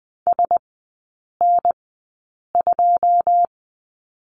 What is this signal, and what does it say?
Morse code, heard as a keyed tone, about 20 wpm, spaced as for 8 wpm, 700 Hz. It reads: SN2
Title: 087 Lesson 15e - Number 5 - Three Characters 20wpm